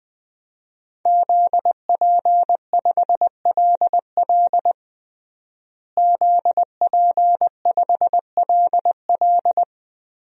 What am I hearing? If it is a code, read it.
Morse code, heard as a keyed tone, about 21 wpm, 700 Hz. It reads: ZP5LL ZP5LL
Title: DXCC-100 Call Signs - 5 Characters - Repeated 3x - 20wpm